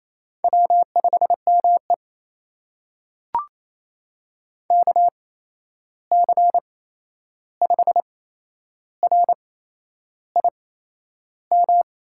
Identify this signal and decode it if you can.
Morse code, heard as a keyed tone, about 27 wpm, spaced as for 10 wpm, 700 Hz. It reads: W5ME KC5RIM
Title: USA Call Signs - 8x Character Spacing - 28wpm